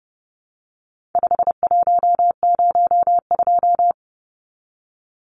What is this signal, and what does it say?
Morse code, heard as a keyed tone, about 30 wpm, 700 Hz. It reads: 5102
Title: Numbers - 4-Digit 30wpm